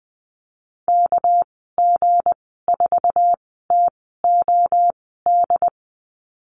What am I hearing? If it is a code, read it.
Morse code, heard as a keyed tone, about 19 wpm, spaced as for 14 wpm, 700 Hz. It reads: KG4TOD
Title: 207 Lesson 33g - Letter j - Call Signs 20wpm